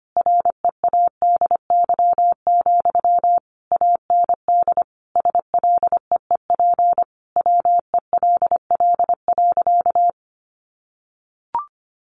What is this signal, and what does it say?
Morse code, heard as a keyed tone, about 25 wpm, 700 Hz. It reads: READY, AND SLEEP WELL.
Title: Sentences from Top 500 Words 25wpm